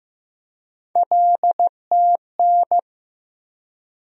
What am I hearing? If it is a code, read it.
Morse code, heard as a keyed tone, about 15 wpm, 700 Hz. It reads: LTN
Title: USA Call Sign Postfixes - 3 Letters 15wpm